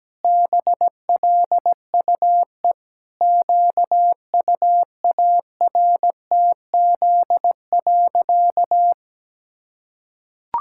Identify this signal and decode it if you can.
Morse code, heard as a keyed tone, about 17 wpm, 700 Hz. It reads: BLUE QUARTZ.
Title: Pangram Challenge v2 - 17wpm